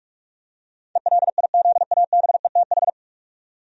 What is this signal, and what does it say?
Morse code, heard as a keyed tone, about 46 wpm, 700 Hz. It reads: ELIZABETH
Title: USA Common Names - 45wpm